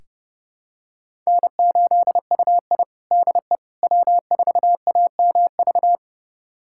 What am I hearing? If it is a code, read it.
Morse code, heard as a keyed tone, about 30 wpm, 700 Hz. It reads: N8UI DE W4AMV
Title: USA Callsign DE Callsign 30wpm